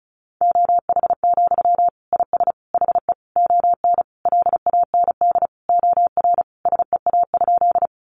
Morse code, heard as a keyed tone, about 34 wpm, 700 Hz. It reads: OH, IS HE ON LAND OR SEA?
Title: Sentences from Top 500 Words 35wpm